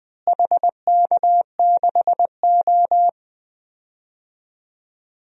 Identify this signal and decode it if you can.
Morse code, heard as a keyed tone, about 20 wpm, 700 Hz. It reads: HK6O
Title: DXCC-100 Call Signs - 4 Characters - Repeated 2x - 20wpm